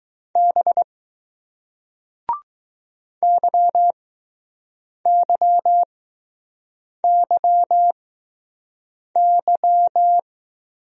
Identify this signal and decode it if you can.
Morse code, heard as a keyed tone, about 20 wpm, spaced as for 6 wpm, 700 Hz. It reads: B YYYY